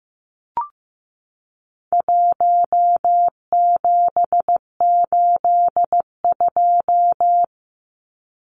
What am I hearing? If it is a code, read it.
Morse code, heard as a keyed tone, about 15 wpm, 700 Hz. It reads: 1782
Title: Numbers - 4-Digit 15wpm